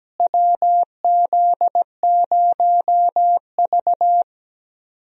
Morse code, heard as a keyed tone, about 17 wpm, 700 Hz. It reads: WZ0V
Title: USA Call Signs - 4 Characters - Repeated 3x - 17wpm